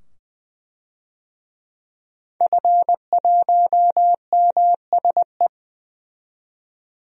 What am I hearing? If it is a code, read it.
Morse code, heard as a keyed tone, about 20 wpm, 700 Hz. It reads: F1MSE
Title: DXCC-100 Call Signs - 5 Characters - 20wpm